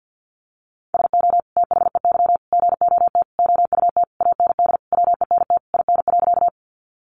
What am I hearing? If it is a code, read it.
Morse code, heard as a keyed tone, about 52 wpm, 700 Hz. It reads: SO THEY GOT OUT AND WENT IN.
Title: Sentences from Top 300 Words 50wpm